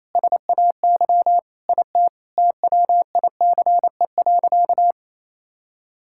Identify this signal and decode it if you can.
Morse code, heard as a keyed tone, about 28 wpm, 700 Hz. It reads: SAY IT TWICE.
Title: Sentences from Top 1600 Words 28wpm